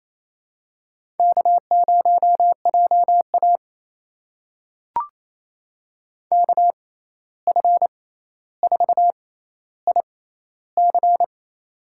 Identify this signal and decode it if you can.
Morse code, heard as a keyed tone, about 28 wpm, spaced as for 13 wpm, 700 Hz. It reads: K0JA KF4IC
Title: USA Call Signs - 6x Character Spacing - 28wpm